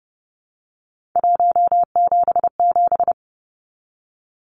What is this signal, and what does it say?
Morse code, heard as a keyed tone, about 30 wpm, 700 Hz. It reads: 177
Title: Numbers - 3-Digit 30wpm